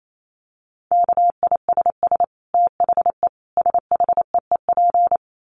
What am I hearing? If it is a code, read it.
Morse code, heard as a keyed tone, about 28 wpm, 700 Hz. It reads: KISS THE SHEEP
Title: Dit Challenge 28wpm